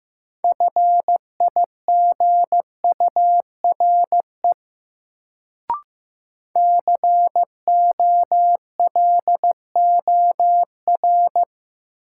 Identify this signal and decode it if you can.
Morse code, heard as a keyed tone, about 15 wpm, 700 Hz. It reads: FIGURE COLOR
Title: Sentences from Top 1000 Words - Review Part 2 - New Word Drill - 15wpm